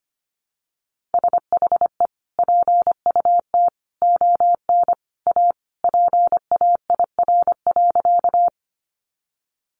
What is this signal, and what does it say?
Morse code, heard as a keyed tone, about 25 wpm, 700 Hz. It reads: SHE PUT ON A PAIR.
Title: Sentences from Top 1200 Words 25wpm